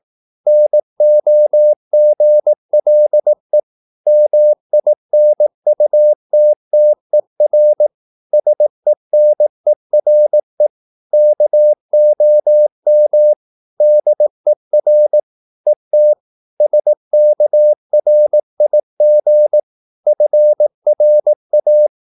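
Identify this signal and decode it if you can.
Morse code, heard as a keyed tone, about 18 wpm, 600 Hz. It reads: NOGLE MINUTTER SENERE KOM DER ET SKRIG FRA
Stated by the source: Slow (QRS) Morse practice at 18 wpm, CW med Gnister.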